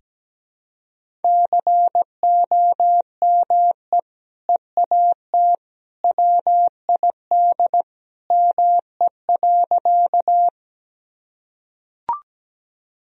Morse code, heard as a keyed tone, about 17 wpm, 700 Hz. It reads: COME EAT WID ME.
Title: Sentences from Top 300 Words and Abbreviations 17wpm